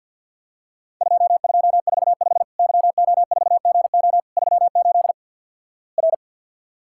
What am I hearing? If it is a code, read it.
Morse code, heard as a keyed tone, about 50 wpm, 700 Hz. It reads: 2245 YO4GO 38 R
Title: Worked All Europe DX Contest - QTC Exchange with Repeats - 50wpm